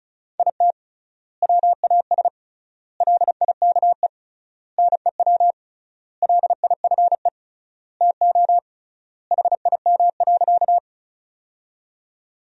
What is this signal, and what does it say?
Morse code, heard as a keyed tone, about 36 wpm, 700 Hz. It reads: IT WAS LIKE NEW LIFE TO HIM.
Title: 3x Word-Spacing - Sentences from Top 300 Words - 35wpm